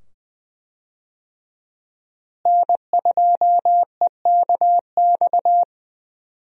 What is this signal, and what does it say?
Morse code, heard as a keyed tone, about 20 wpm, 700 Hz. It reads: N2EKX
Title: USA Call Signs - 5 Characters - Repeated 2x - 20wpm